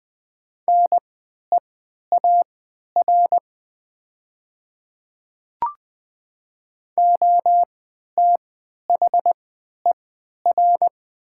Morse code, heard as a keyed tone, about 20 wpm, spaced as for 11 wpm, 700 Hz. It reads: NEAR OTHER
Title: 3x Character Spacing - Top 300 Words - 20wpm